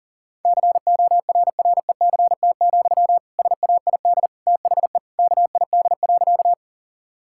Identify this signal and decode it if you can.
Morse code, heard as a keyed tone, about 37 wpm, 700 Hz. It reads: CORRECT, SAID THE KID.
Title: Sentences from Top 1600 Words 40wpm